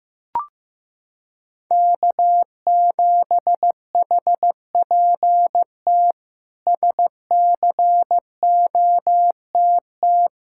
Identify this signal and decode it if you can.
Morse code, heard as a keyed tone, about 16 wpm, 700 Hz. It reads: K7HPT SCOTT
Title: K1USN SST QSO Practice - 15wpm